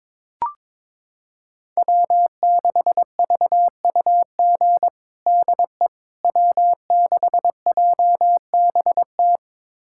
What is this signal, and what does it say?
Morse code, heard as a keyed tone, about 22 wpm, 700 Hz. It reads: W6VUG DE W6JBT
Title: USA Callsign DE Callsign - 5 Characters - 22wpm